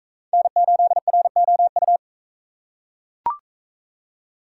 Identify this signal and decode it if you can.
Morse code, heard as a keyed tone, about 40 wpm, 700 Hz. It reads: N8ROU